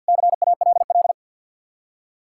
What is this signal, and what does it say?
Morse code, heard as a keyed tone, about 64 wpm, 700 Hz. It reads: CALL